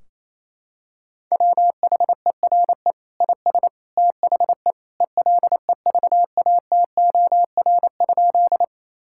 Morse code, heard as a keyed tone, about 28 wpm, 700 Hz. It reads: WHERE IS THE ELEVATOR?
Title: Sentences from Top 1600 Words 28wpm